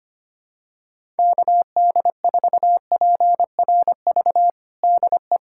Morse code, heard as a keyed tone, about 25 wpm, 700 Hz. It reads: KD4PRV DE
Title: USA Callsign DE Callsign - 6 Characters - 25wpm